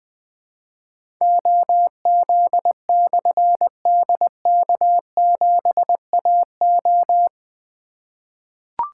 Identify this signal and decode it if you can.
Morse code, heard as a keyed tone, about 21 wpm, 700 Hz. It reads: OZ/DK7AO
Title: Difficult Call Signs 20wpm